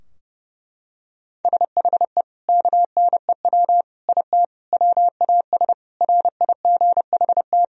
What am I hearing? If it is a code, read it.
Morse code, heard as a keyed tone, about 32 wpm, 700 Hz. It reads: SHE KNEW IT WAS RIGHT
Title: Sentences from Top 1000 Words - Review Part 2 - 30wpm